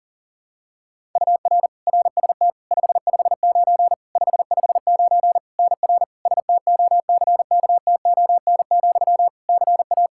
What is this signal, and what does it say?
Morse code, heard as a keyed tone, about 40 wpm, 700 Hz. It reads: UR RST 559 559 NR STOCKTON, CA
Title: Practice QSO - Exchange 1 - Beginning - 40wpm